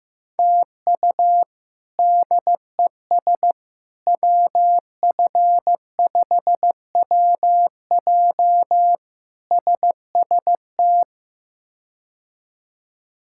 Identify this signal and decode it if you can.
Morse code, heard as a keyed tone, about 16 wpm, 700 Hz. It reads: TU DES WF5WJ SST
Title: K1USN SST QSO Practice - 15wpm